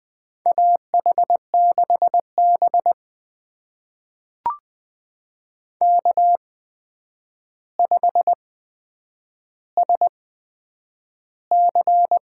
Morse code, heard as a keyed tone, about 20 wpm, spaced as for 9 wpm, 700 Hz. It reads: AH6B K5SC
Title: USA Call Signs - 4 Characters - 8x Character Spacing - 20wpm